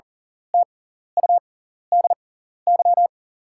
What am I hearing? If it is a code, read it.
Morse code, heard as a keyed tone, about 40 wpm, spaced as for 14 wpm, 690 Hz. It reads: TUDY
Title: 6x Character Spacing - Top 300 Words - 40wpm